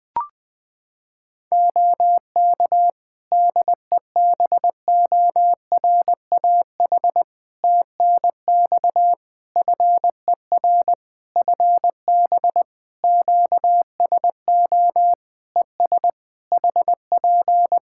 Morse code, heard as a keyed tone, about 20 wpm, 700 Hz. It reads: OK DEBORAH TNX FER FB QSO ES HP
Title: Practice QSO - Exchange 4 - Ending - 20wpm